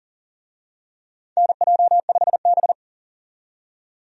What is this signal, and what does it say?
Morse code, heard as a keyed tone, about 37 wpm, 700 Hz. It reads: NJ5B
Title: USA Call Signs - 4 Characters - 40wpm